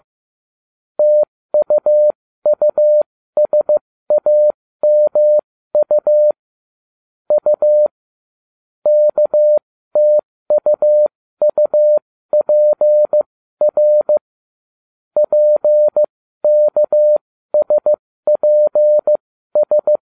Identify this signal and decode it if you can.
Morse code, heard as a keyed tone, about 15 wpm, spaced as for 13 wpm, 600 Hz. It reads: TUUSAMU U KTUUPR PKSPS